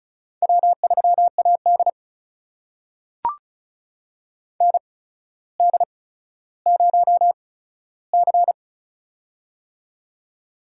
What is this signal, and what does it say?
Morse code, heard as a keyed tone, about 36 wpm, spaced as for 15 wpm, 700 Hz. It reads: W3AD ND0C
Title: USA Call Signs - 4 Characters - 8x Character Spacing - 35wpm